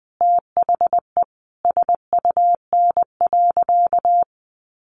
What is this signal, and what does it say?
Morse code, heard as a keyed tone, about 20 wpm, 700 Hz. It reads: THE SUN.
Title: Sentences from Top 500 Words 20wpm